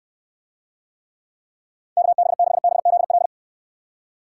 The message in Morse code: KB6BZL

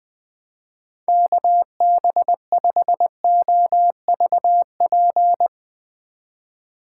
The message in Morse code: KB5OVP

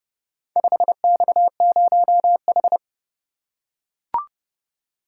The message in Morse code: 5X0H

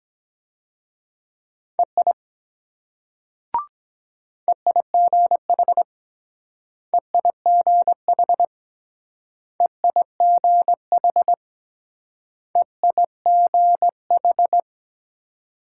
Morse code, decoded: EI EIGH EIGH EIGH EIGH

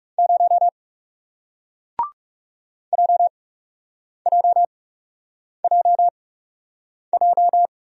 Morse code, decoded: 0 JJJJ

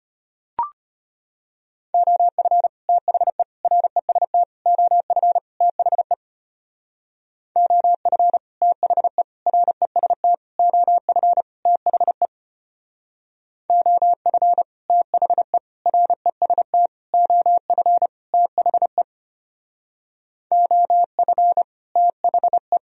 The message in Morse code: OF THE REST OF THE OF THE REST OF THE OF THE REST OF THE OF THE

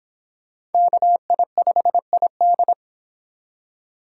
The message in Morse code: KI5ID

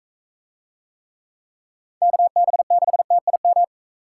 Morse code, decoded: KB6TIM

